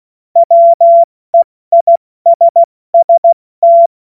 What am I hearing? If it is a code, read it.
Morse code, heard as a keyed tone, about 16 wpm, 680 Hz. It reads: WEISST
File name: Witz-28-Oktober-2025-70-F80